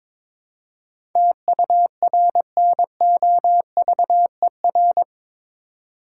Morse code, heard as a keyed tone, about 22 wpm, 700 Hz. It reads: TURNOVER
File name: practice / 22wpm / Eight Letter Words - 22wpm